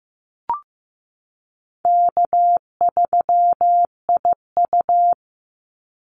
Morse code, heard as a keyed tone, about 15 wpm, 700 Hz. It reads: K3IU
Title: Easy Call Signs Repeated 3x - 15wpm